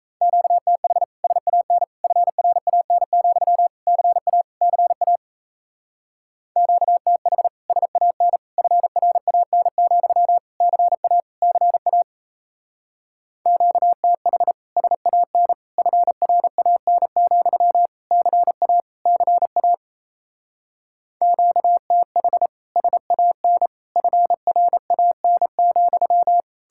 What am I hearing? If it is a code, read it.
Morse code, heard as a keyed tone, about 38 wpm, 700 Hz. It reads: QTH SAN FRAN, CA CA QTH SAN FRAN, CA CA QTH SAN FRAN, CA CA QTH SAN FRAN,